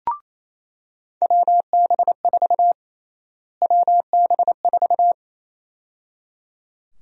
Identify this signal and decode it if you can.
Morse code, heard as a keyed tone, about 28 wpm, 700 Hz. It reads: WB4 WB4
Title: USA Call Sign Prefixes - Repeated 2x - 28wpm